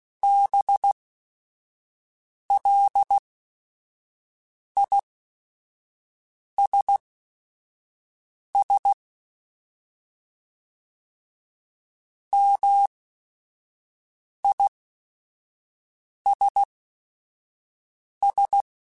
Morse code, heard as a keyed tone, about 16 wpm, spaced as for 5 wpm, 800 Hz. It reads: BLISS MISS